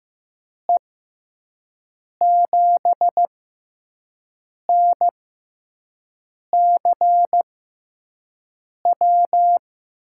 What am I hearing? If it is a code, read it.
Morse code, heard as a keyed tone, about 15 wpm, spaced as for 5 wpm, 700 Hz. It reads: E7NCW
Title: USA Call Signs - 6 Characters - 6x Character Spacing - 15wpm